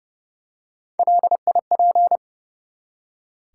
Morse code, heard as a keyed tone, about 30 wpm, 700 Hz. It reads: LIP
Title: Three Letter Words - 30wpm